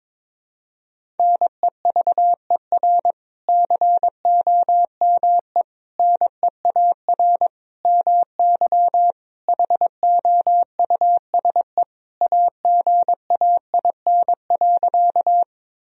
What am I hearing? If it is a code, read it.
Morse code, heard as a keyed tone, about 22 wpm, 700 Hz. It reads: NEVER COME NEAR MY HOUSE AGAIN.
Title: Sentences from Top 1000 Words - Review Part 1 - 22wpm